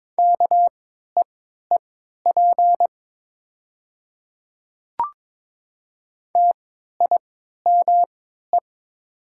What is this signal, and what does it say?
Morse code, heard as a keyed tone, about 22 wpm, spaced as for 12 wpm, 700 Hz. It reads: KEEP TIME
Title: 3x Character Spacing - Top 300 Words - 22wpm